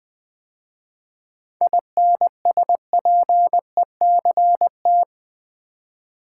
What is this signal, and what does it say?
Morse code, heard as a keyed tone, about 20 wpm, 690 Hz. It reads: INSPECT
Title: 7 Letter Words - First 3 Letters Given - 20wpm